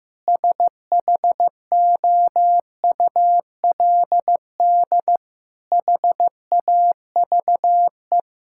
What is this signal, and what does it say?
Morse code, heard as a keyed tone, about 15 wpm, 700 Hz. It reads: SHOULD HAVE
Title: Sentences from Top 1000 Words - Review Part 6 - 15wpm